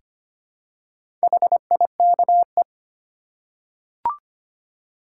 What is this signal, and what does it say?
Morse code, heard as a keyed tone, about 25 wpm, 700 Hz. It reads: HIKE
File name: practice / 25wpm / Easy Words with Hints - 25wpm